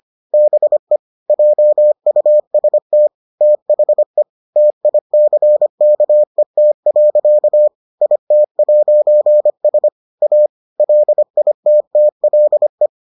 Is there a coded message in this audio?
Yes: Morse code from 0.3 to 12.9 s, heard as a keyed tone, about 25 wpm, 600 Hz. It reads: BE JUST THE TICKET. IT'S A LITTLE